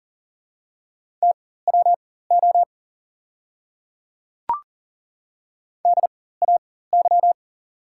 Morse code, heard as a keyed tone, about 38 wpm, spaced as for 18 wpm, 700 Hz. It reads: TWO DAY
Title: 4x Character Spacing - Top 100 Words - 40wpm